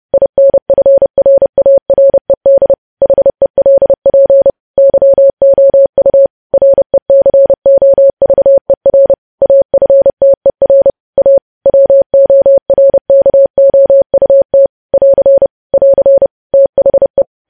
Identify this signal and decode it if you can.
Morse code, heard as a keyed tone, about 30 wpm, 550 Hz. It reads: INFRARED HELP YOU RECOVER AFTER A WORKOUT <AR> <AR> THE